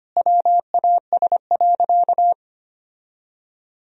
Morse code, heard as a keyed tone, about 25 wpm, 700 Hz. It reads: WAS.
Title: Sentences from Top 1000 Words - Review Part 3 - 25wpm